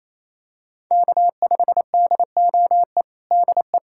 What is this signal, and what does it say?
Morse code, heard as a keyed tone, about 28 wpm, 700 Hz. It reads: K5DOE DE
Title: USA Callsign DE Callsign - 5 Characters - 28wpm